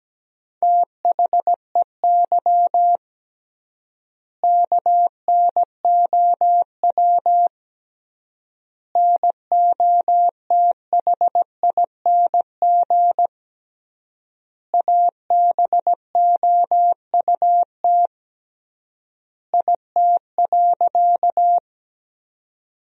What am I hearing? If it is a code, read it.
Morse code, heard as a keyed tone, about 17 wpm, 700 Hz. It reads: THEY KNOW NOTHING ABOUT IT.
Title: 3x Word-Spacing - Sentences from Top 500 Words - 17wpm